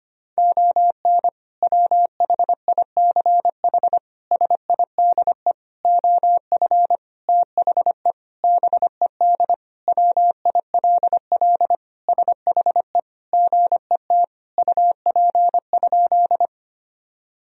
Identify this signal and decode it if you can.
Morse code, heard as a keyed tone, about 25 wpm, 700 Hz. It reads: ON WHICH SIDE OF THE BED WILL SHE GET UP?
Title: Sentences from Top 1000 Words - Review Part 1 - 25wpm